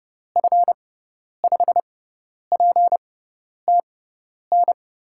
Morse code, heard as a keyed tone, about 30 wpm, spaced as for 10 wpm, 700 Hz. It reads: F5PTN
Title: USA Call Signs - 6 Characters - 6x Character Spacing - 30wpm